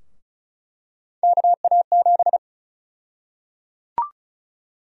Morse code, heard as a keyed tone, about 34 wpm, 700 Hz. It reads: KA7
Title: USA Call Sign Prefixes - 35wpm